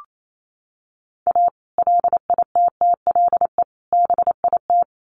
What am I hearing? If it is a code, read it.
Morse code, heard as a keyed tone, about 28 wpm, 700 Hz. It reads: A LITTLE BIT
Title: Sets of 3 Words 28wpm